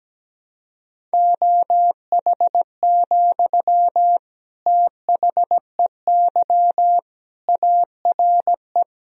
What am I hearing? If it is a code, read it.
Morse code, heard as a keyed tone, about 17 wpm, 700 Hz. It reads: OH, THEY ARE
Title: Sentences from Top 1000 Words - Review Part 4 - 17wpm